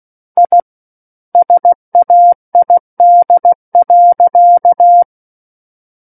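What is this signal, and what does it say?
Morse code, heard as a keyed tone, about 16 wpm, 700 Hz. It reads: I SAID.
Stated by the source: Morse practice of book text at 16 wpm.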